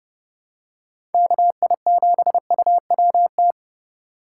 Morse code, heard as a keyed tone, about 30 wpm, 700 Hz. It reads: KI7UWT